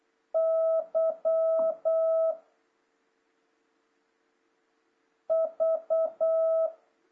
Morse code, heard as a keyed tone, about 8 wpm, 660 Hz. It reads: Y V
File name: jh8cbh / 2026cw